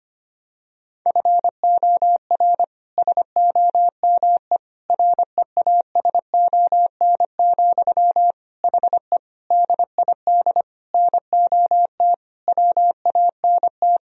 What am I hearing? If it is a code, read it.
Morse code, heard as a keyed tone, about 25 wpm, 700 Hz. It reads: FOR SOME REASON, HE DID NOT WANT
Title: Sentences from Top 1000 Words - Review Part 3 - 25wpm